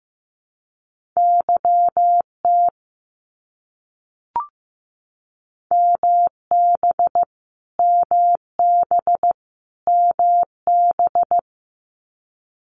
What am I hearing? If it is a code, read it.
Morse code, heard as a keyed tone, about 15 wpm, 700 Hz. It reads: YT MB MB MB
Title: Canadian Provinces and Territories - Repeated 3x - 15wpm